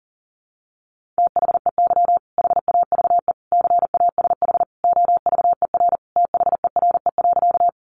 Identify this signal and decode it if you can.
Morse code, heard as a keyed tone, about 40 wpm, 700 Hz. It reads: THEY HAVE CASH OVER THERE.
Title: Sentences from Top 700 Words 40wpm